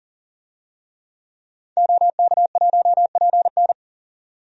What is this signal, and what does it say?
Morse code, heard as a keyed tone, about 40 wpm, 700 Hz. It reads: OK1PN
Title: DXCC-100 Call Signs - 5 Characters - 40wpm